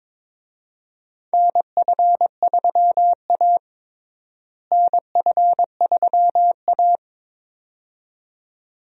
Morse code, heard as a keyed tone, about 21 wpm, 700 Hz. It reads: NF3A NF3A
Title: USA Call Signs - 4 Characters - Repeated 2x - 22wpm